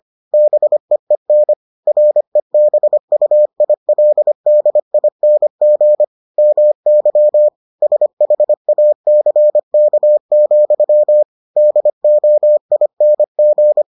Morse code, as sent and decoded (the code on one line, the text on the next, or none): -... . -. / .-. -... ..- .. .-.. -.. .. -. --. / -- -.-- / ... .... .- -.-. -.- --..-- / -.. --- .. -. --.
BEEN REBUILDING MY SHACK, DOING